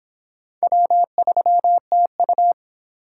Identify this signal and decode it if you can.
Morse code, heard as a keyed tone, about 25 wpm, 700 Hz. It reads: W3TU